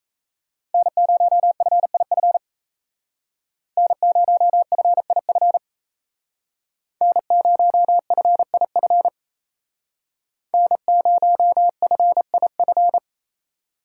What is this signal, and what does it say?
Morse code, heard as a keyed tone, about 35 wpm, 700 Hz. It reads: N0FIF N0FIF N0FIF N0FIF
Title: Speed Racing - USA 1x3 Call Signs 28wpm